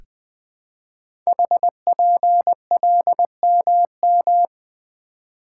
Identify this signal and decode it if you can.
Morse code, heard as a keyed tone, about 21 wpm, 700 Hz. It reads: HPLMM
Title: Code Groups - 5 Letters - 20wpm